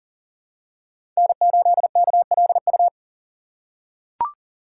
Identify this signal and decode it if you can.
Morse code, heard as a keyed tone, about 38 wpm, 700 Hz. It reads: N8KLU
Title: USA Call Signs - 5 Characters - 3x Character Spacing - 40wpm